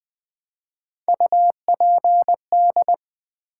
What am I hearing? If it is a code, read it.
Morse code, heard as a keyed tone, about 20 wpm, 700 Hz. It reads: UPD
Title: Build A Word - 3 Letters At A Time - 6-Letter Words - 20wpm